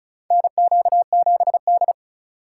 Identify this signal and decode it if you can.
Morse code, heard as a keyed tone, about 34 wpm, 690 Hz. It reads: NQ7D
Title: USA Call Signs - 4 Characters - 2x Character Spacing - 35wpm